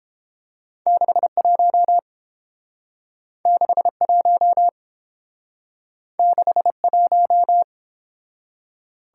Morse code, heard as a keyed tone, about 29 wpm, 700 Hz. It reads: 61 61 61